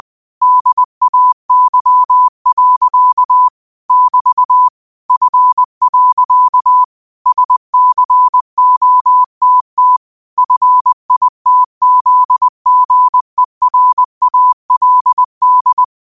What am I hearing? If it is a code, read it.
Morse code, heard as a keyed tone, about 20 wpm, 1000 Hz. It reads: DAY. <BT> F. SCOTT FITZGERALD